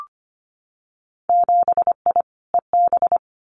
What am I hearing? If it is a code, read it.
Morse code, heard as a keyed tone, about 25 wpm, 700 Hz. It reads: 7I EB
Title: Winter Field Day Exchanges Encore 25wpm